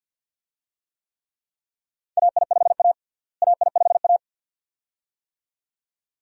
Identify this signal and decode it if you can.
Morse code, heard as a keyed tone, about 50 wpm, 700 Hz. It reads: AI5A AI5A
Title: USA Call Signs - 4 Characters - Repeated 2x - 50wpm